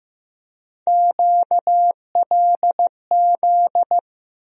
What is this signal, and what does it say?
Morse code, heard as a keyed tone, about 15 wpm, 700 Hz. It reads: QLZ
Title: USA Call Sign Postfixes - 3 Letters 15wpm